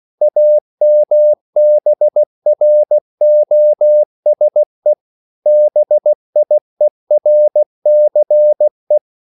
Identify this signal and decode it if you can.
Morse code, heard as a keyed tone, about 16 wpm, 600 Hz. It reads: AMBROSE BIERCE
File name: CW / Practice / EnglishQuotations0416